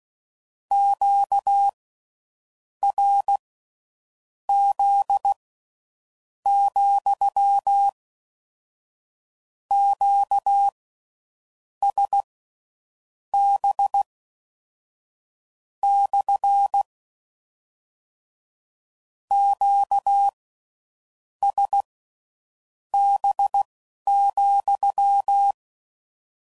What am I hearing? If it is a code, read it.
Morse code, heard as a keyed tone, about 16 wpm, spaced as for 7 wpm, 800 Hz. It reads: QRZ, QSB / QSB,